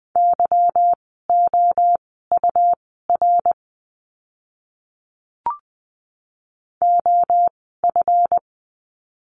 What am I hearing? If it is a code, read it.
Morse code, heard as a keyed tone, about 20 wpm, spaced as for 14 wpm, 700 Hz. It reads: YOUR OF
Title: 206 Lesson 33f - Letter j - Words 20wpm